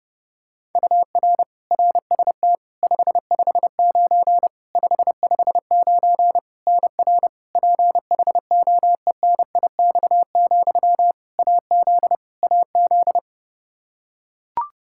Morse code, heard as a keyed tone, about 32 wpm, 700 Hz. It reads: UR RST 559 559 NR PHOENIX, AZ AZ
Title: Practice QSO - Exchange 1 - Beginning - 30wpm